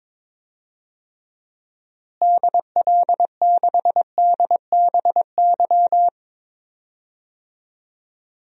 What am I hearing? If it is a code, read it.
Morse code, heard as a keyed tone, about 22 wpm, 700 Hz. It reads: DL6DBY
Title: DXCC-100 Call Signs - 6 Characters - Repeated 3x - 22wpm